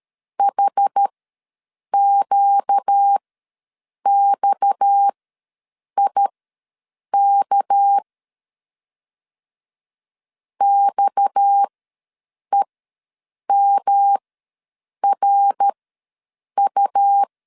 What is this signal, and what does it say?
Morse code, heard as a keyed tone, about 13 wpm, spaced as for 7 wpm, 800 Hz. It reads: HQXIK XEMRU
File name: LEZIONI / 6 PKX0 / 6 PKX ws 12wpm 3106